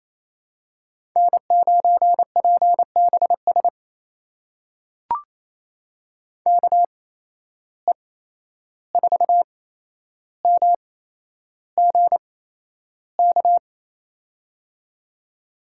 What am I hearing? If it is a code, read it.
Morse code, heard as a keyed tone, about 28 wpm, spaced as for 11 wpm, 700 Hz. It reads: N9PBS KE4MGK
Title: USA Call Signs - 8x Character Spacing - 28wpm